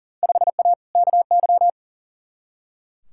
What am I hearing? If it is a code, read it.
Morse code, heard as a keyed tone, about 40 wpm, 700 Hz. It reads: 5A KY
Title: Field Day Exchanges - 40wpm